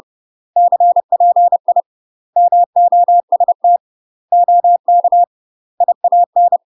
Morse code, heard as a keyed tone, about 30 wpm, 700 Hz. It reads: CPI MOST OK IAN
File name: UK MASTER QSOs TRAINING / 30WPM / UK MASTER QSOs 0050